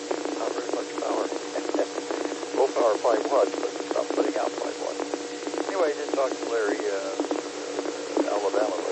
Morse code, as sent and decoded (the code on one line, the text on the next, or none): none